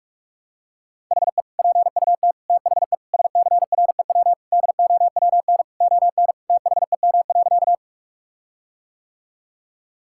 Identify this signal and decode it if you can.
Morse code, heard as a keyed tone, about 46 wpm, 700 Hz. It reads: HE PUT THE SCREW DOWN ON THEM.
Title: Sentences from Top 1600 Words 45wpm